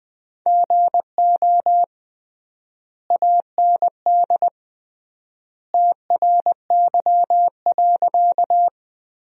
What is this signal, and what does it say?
Morse code, heard as a keyed tone, about 20 wpm, 700 Hz. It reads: GO AND TRY.
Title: 3x Word-Spacing - Sentences from Top 200 Words - 20wpm